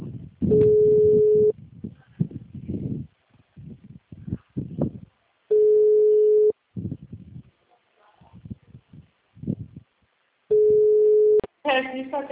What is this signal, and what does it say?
Ringback tone (Europe, 425 Hz), with speech.
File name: out- -2031- - - .4